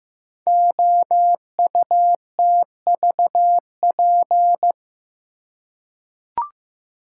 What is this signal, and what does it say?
Morse code, heard as a keyed tone, about 15 wpm, 700 Hz. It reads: OUTVP
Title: Code Groups - 5 Letter:Numbers - 15wpm